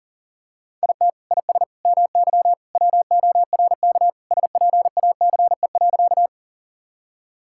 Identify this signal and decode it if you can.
Morse code, heard as a keyed tone, about 40 wpm, 700 Hz. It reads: IT IS MY WORK SPACE.
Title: Sentences from Top 1000 Words - Review Part 5 - 40wpm